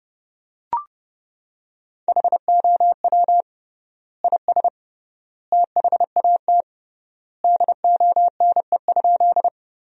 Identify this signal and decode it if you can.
Morse code, heard as a keyed tone, about 30 wpm, 700 Hz. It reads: HOW IS THAT DONE?
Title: 3x Word-Spacing - Sentences from Top 500 Words - 30wpm